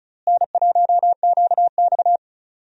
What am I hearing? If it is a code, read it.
Morse code, heard as a keyed tone, about 36 wpm, 700 Hz. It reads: N1QX
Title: USA Call Signs - 4 Characters - Repeated 3x - 35wpm